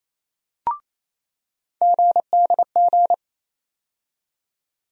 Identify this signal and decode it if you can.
Morse code, heard as a keyed tone, about 28 wpm, 700 Hz. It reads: GDG
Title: USA Call Sign Postfixes - 3 Letters 28wpm